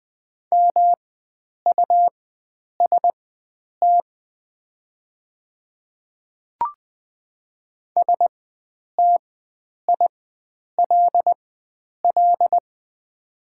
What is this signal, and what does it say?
Morse code, heard as a keyed tone, about 20 wpm, spaced as for 9 wpm, 700 Hz. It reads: MUST STILL
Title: 4x Character Spacing - Top 500 Words - 20wpm